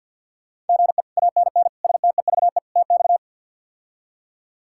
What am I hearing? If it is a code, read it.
Morse code, heard as a keyed tone, about 46 wpm, 700 Hz. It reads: GE ANN STEVE TX
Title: K1USN SST QSO Practice - 50wpm